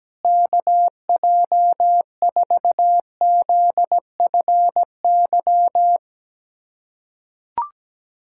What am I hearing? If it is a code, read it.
Morse code, heard as a keyed tone, about 17 wpm, 700 Hz. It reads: KJ4ZFY
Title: USA Callsign DE Callsign - 6 Characters - 17wpm